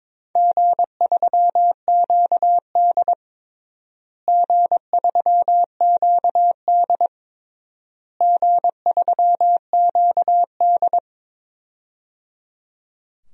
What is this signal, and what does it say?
Morse code, heard as a keyed tone, about 22 wpm, 700 Hz. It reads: G3QD G3QD G3QD
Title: DXCC-100 Call Signs - 4 Characters - Repeated 3x - 22wpm